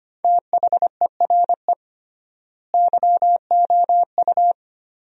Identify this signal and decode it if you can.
Morse code, heard as a keyed tone, about 25 wpm, 700 Hz. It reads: THERE YOU
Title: 3x Word-Spacing - Sentences from Top 100 Words - 25wpm